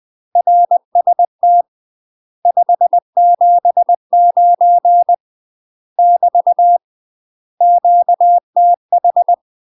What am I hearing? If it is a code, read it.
Morse code, heard as a keyed tone, about 20 wpm, 700 Hz. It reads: RST 579 <BT> QTH
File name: UK MASTER QSOs TRAINING / 20WPM / UK MASTER QSOs 0008